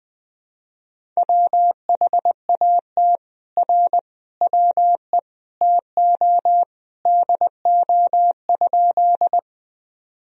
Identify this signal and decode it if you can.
Morse code, heard as a keyed tone, about 20 wpm, 700 Hz. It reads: WHAT R WE TO DO?
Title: Sentences from Top 100 Words and Abbreviations 20wpm